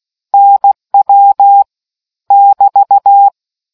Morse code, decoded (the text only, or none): NW <BT>